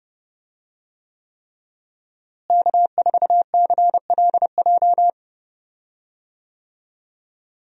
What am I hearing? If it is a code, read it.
Morse code, heard as a keyed tone, about 30 wpm, 700 Hz. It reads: K4CLJ
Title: USA Call Signs - 5 Characters - Repeated 3x - 30wpm